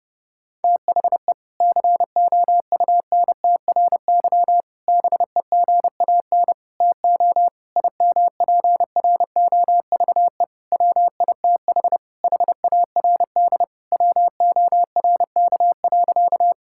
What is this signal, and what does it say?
Morse code, heard as a keyed tone, about 32 wpm, 700 Hz. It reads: THE COUNTRY BEGAN TO IMPROVE WITH HARD WORK.
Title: Sentences from Top 1000 Words - Review Part 5 - 30wpm